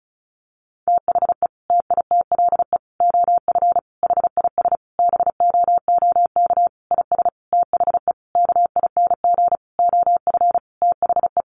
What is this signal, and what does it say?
Morse code, heard as a keyed tone, about 34 wpm, 700 Hz. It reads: THE TITLE OF HIS BOOK IS THE KING OF THE
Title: Sentences from Top 1200 Words 35wpm